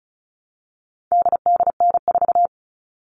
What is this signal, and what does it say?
Morse code, heard as a keyed tone, about 34 wpm, 700 Hz. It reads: DDN4
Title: All Cut Numbers - 4-Digit 35wpm